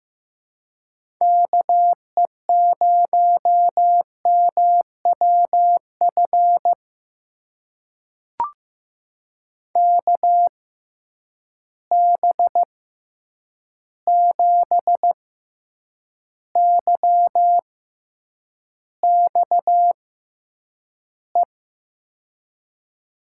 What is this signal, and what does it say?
Morse code, heard as a keyed tone, about 15 wpm, spaced as for 8 wpm, 700 Hz. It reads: KE0MWF KB7YXE
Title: USA Call Signs - 6x Character Spacing - 15wpm